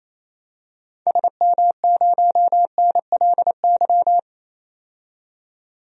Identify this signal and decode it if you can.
Morse code, heard as a keyed tone, about 28 wpm, 700 Hz. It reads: SM0NLY
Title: DXCC-100 Call Signs - 6 Characters - 28wpm